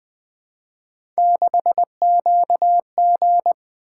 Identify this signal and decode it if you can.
Morse code, heard as a keyed tone, about 20 wpm, 700 Hz. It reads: 6QG
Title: Three Letter-Number Combinations - 20wpm